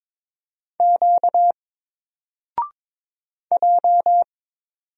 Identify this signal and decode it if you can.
Morse code, heard as a keyed tone, about 22 wpm, 700 Hz. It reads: Q J
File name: Single-Letter-Number-22wpm